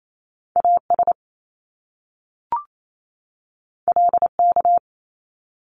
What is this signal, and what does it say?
Morse code, heard as a keyed tone, about 28 wpm, 700 Hz. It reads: AS LK
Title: Top Two Letter Combinations in Words 28wpm